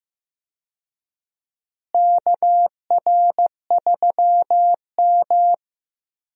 Morse code, heard as a keyed tone, about 15 wpm, 700 Hz. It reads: KR3M
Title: USA Call Signs - 4 Characters - Repeated 3x - 15wpm